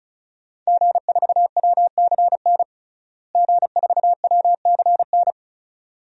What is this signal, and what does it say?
Morse code, heard as a keyed tone, about 36 wpm, 700 Hz. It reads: G4WCN G4WCN
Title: DXCC-100 Call Signs - 5 Characters - Repeated 2x - 35wpm